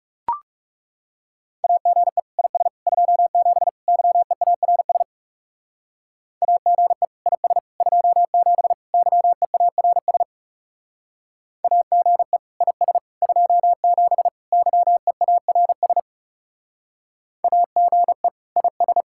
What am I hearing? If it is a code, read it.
Morse code, heard as a keyed tone, about 37 wpm, 700 Hz. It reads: AGE IS 27 YEARS AGE IS 27 YEARS AGE IS 27 YEARS AGE IS
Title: Speed-Racing Common QSO Phrases - 30wpm